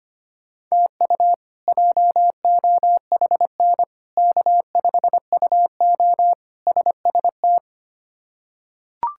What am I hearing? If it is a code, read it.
Morse code, heard as a keyed tone, about 25 wpm, 700 Hz. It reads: TU JOHN K5UO SST
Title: K1USN SST QSO Practice - 25wpm